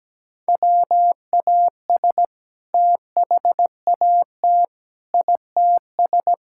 Morse code, heard as a keyed tone, about 17 wpm, 700 Hz. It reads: WAS THAT ITS
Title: Sentences from Top 1000 Words - Review Part 5 - 17wpm